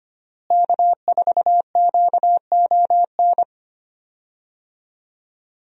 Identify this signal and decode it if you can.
Morse code, heard as a keyed tone, about 25 wpm, 700 Hz. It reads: K4QON
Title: USA Call Signs - 5 Characters - Repeated 2x - 25wpm